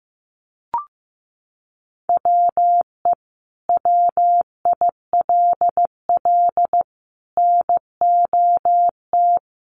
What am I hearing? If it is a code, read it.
Morse code, heard as a keyed tone, about 15 wpm, 700 Hz. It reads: WE WILL NOT
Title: Sentences from Top 1300 Words 15wpm